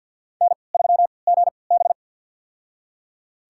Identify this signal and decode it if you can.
Morse code, heard as a keyed tone, about 46 wpm, spaced as for 27 wpm, 700 Hz. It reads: N3GB
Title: USA Call Signs - 4 Characters - 3x Character Spacing - 50wpm